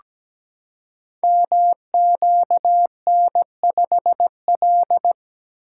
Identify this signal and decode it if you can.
Morse code, heard as a keyed tone, about 17 wpm, 690 Hz. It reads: MQN5L
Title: Code Groups - 5 Letter:Numbers - 17wpm